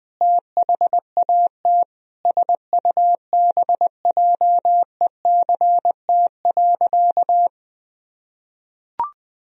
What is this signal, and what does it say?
Morse code, heard as a keyed tone, about 21 wpm, 700 Hz. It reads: THAT SUBJECT.
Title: Sentences from Top 1000 Words - Review Part 1 - 20wpm